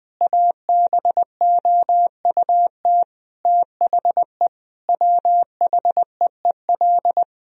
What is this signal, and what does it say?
Morse code, heard as a keyed tone, about 20 wpm, 700 Hz. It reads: ABOUT THE WHEEL
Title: Sentences from Top 1000 Words - Review Part 3 - 20wpm